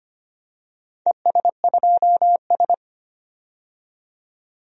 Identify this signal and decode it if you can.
Morse code, heard as a keyed tone, about 25 wpm, 700 Hz. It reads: ES2S
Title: DXCC-100 Call Signs - 4 Characters - 25wpm